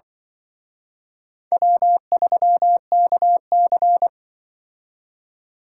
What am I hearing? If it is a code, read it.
Morse code, heard as a keyed tone, about 24 wpm, 700 Hz. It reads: W3KC
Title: Speed Racing - USA Call Signs 20wpm